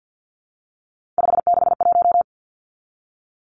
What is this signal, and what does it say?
Morse code, heard as a keyed tone, about 48 wpm, 700 Hz. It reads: 561
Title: Numbers - 3-Digit 50wpm